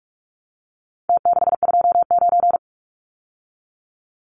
Morse code, heard as a keyed tone, about 44 wpm, 700 Hz. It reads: T629
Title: Numbers - Contest Style - 4-Digit 45wpm